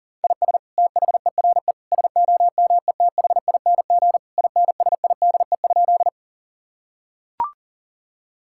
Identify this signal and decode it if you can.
Morse code, heard as a keyed tone, about 37 wpm, 700 Hz. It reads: IS THERE SOMETHING INSIDE?
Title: Sentences from Top 1000 Words - Review Part 6 - 40wpm